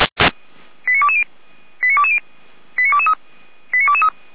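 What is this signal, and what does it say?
Five-tone selective call (ZVEI), heard as quick runs of five tones, tones of about 70 ms. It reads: 89220 89220 89223 89223
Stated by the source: Five-tone ZVEI selective call, code 89163, received by SDR.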